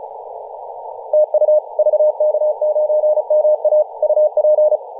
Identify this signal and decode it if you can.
Morse code, heard as a keyed tone, about 34 wpm, 610 Hz. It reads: TU VK9MA UP
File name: posts / 2017-nov-04-2233z-vk9ma-21-025mhz-cw